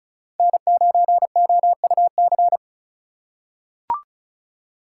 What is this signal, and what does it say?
Morse code, heard as a keyed tone, about 34 wpm, 700 Hz. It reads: N9OUC
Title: USA Call Signs - 5 Characters - 4x Character Spacing - 35wpm